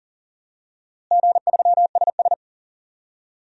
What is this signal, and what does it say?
Morse code, heard as a keyed tone, about 40 wpm, 700 Hz. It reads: G3SS
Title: DXCC-100 Call Signs - 4 Characters - Repeated 3x - 40wpm